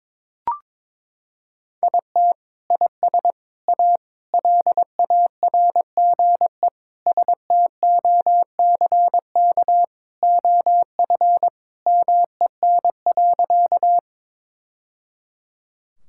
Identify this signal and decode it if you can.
Morse code, heard as a keyed tone, about 22 wpm, 700 Hz. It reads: IT IS A LARGE STOCK OF MEN.
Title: Sentences from Top 1000 Words - Review Part 5 - 22wpm